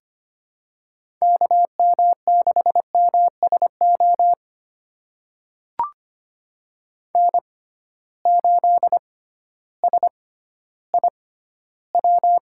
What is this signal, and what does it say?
Morse code, heard as a keyed tone, about 25 wpm, spaced as for 13 wpm, 700 Hz. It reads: KM6MSO N8SIW
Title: USA Call Signs - 6x Character Spacing - 25wpm